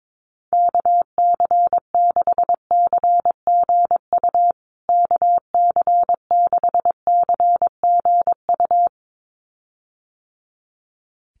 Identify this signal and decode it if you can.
Morse code, heard as a keyed tone, about 22 wpm, 700 Hz. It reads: KC6CGU KC6CGU
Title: USA Call Signs Repeated 2x 22wpm